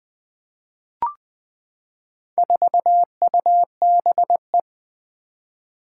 Morse code, heard as a keyed tone, about 20 wpm, 700 Hz. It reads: 4UBE
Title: Four Challenging Letter-Number Combinations 20wpm